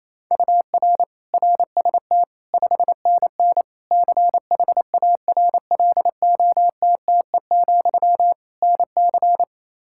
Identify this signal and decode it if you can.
Morse code, heard as a keyed tone, about 28 wpm, 700 Hz. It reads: UR RST 5NN CHARLOTTE, NC
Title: K1USN SST QSO Practice 28wpm